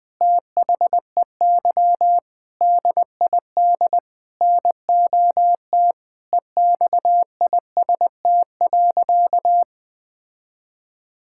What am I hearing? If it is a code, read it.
Morse code, heard as a keyed tone, about 20 wpm, 700 Hz. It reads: THEY DID NOT EXIST.
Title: Sentences from Top 1000 Words - Review Part 6 - 20wpm